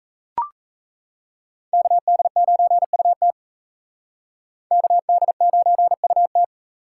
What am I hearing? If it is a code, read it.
Morse code, heard as a keyed tone, about 35 wpm, 700 Hz. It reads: KD9UT KD9UT
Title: Speed Racing - USA 2x2 Call Signs 28wpm